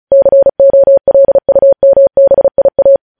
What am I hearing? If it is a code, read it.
Morse code, heard as a keyed tone, about 36 wpm, 550 Hz. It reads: COLUMBIA
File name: podcast / News headlines-1x-35x35